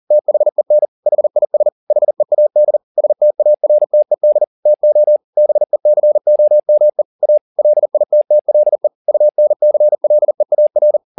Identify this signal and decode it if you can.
Morse code, heard as a keyed tone, about 40 wpm, 600 Hz. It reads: THEN HIS HEAD STARTED TO BECOME A LITTLE UNCLEAR
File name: the old man and the sea 2 40wpm